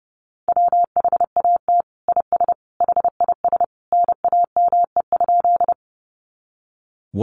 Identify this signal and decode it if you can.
Morse code, heard as a keyed tone, about 30 wpm, 700 Hz. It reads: WHAT IS HIS NAME?
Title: Sentences from Top 200 Words 30wpm